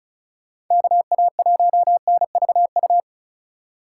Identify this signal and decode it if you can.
Morse code, heard as a keyed tone, about 44 wpm, 700 Hz. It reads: KA1NVU
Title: Speed Racing - USA 2x3 Call Signs 35wpm